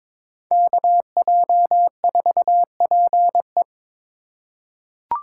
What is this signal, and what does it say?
Morse code, heard as a keyed tone, about 21 wpm, 700 Hz. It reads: KJ4PE
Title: USA Call Signs - 5 Characters - 4x Character Spacing - 22wpm